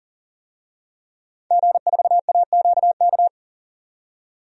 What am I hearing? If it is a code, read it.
Morse code, heard as a keyed tone, about 40 wpm, 700 Hz. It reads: G4AQK
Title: DXCC-100 Call Signs - 5 Characters - 40wpm